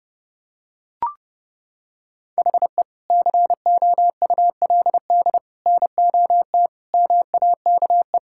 Morse code, heard as a keyed tone, about 32 wpm, 700 Hz. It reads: HE COULD NOT MAKE
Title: Sentences from Top 1600 Words 30wpm